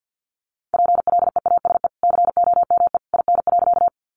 Morse code, heard as a keyed tone, about 50 wpm, 700 Hz. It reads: PLEASE COME IN.
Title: Sentences from Top 1500 Words 50wpm